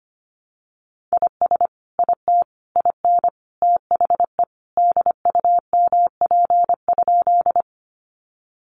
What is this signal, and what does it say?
Morse code, heard as a keyed tone, about 25 wpm, 700 Hz. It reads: IS IT IN THE DUMP?
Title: Sentences from Top 1300 Words 25wpm